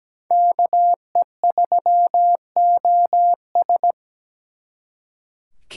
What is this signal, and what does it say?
Morse code, heard as a keyed tone, about 20 wpm, 700 Hz. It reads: KE3OS
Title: Speed Racing - USA 2x2 Call Signs 17wpm